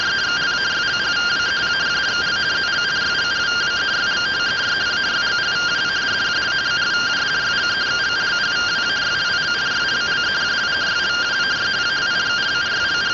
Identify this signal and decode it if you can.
Two-tone teleprinter signal, about 170 Hz shift.
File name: findings / CT AXIAL -256.gen-27 rtty rx